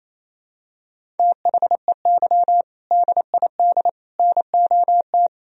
Morse code, heard as a keyed tone, about 28 wpm, 700 Hz. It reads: THEY DID NOT
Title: Sentences from Top 1000 Words - Review Part 1 - 28wpm